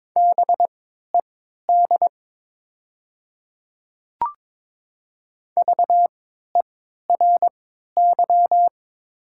Morse code, heard as a keyed tone, about 22 wpm, spaced as for 13 wpm, 700 Hz. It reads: BED VERY
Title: 3x Character Spacing - Top 500 Words - 22wpm